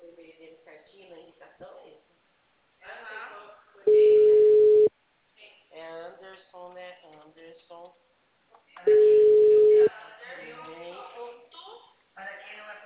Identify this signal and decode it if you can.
Ringback tone (Europe, 425 Hz), with speech.